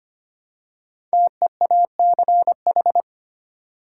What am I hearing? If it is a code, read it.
Morse code, heard as a keyed tone, about 25 wpm, 700 Hz. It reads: TEACH
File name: 5 Letter Words - First 3 Letters Given - 25wpm